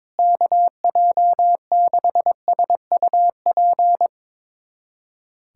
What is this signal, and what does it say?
Morse code, heard as a keyed tone, about 22 wpm, 700 Hz. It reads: KJ6SUP
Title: USA Callsign DE Callsign 22wpm